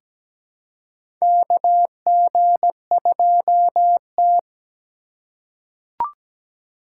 Morse code, heard as a keyed tone, about 17 wpm, 700 Hz. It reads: KG2T
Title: USA Call Signs - 4 Characters - 4x Character Spacing - 17wpm